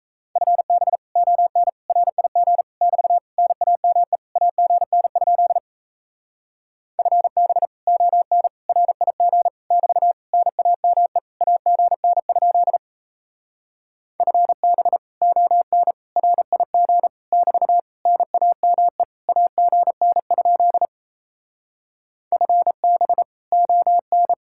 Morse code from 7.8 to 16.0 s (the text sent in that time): ON RIG <BT> NAME AGN? FB ON